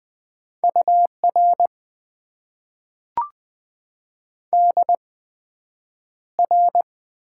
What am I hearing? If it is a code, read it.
Morse code, heard as a keyed tone, about 20 wpm, spaced as for 9 wpm, 700 Hz. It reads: UR DR
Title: Top 90 QSO Elements - 8x Character Spacing - 20wpm